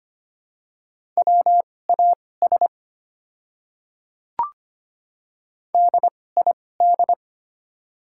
Morse code, heard as a keyed tone, about 25 wpm, spaced as for 18 wpm, 700 Hz. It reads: WAS DID
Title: 2x Character Spacing - Top 200 Words - 25wpm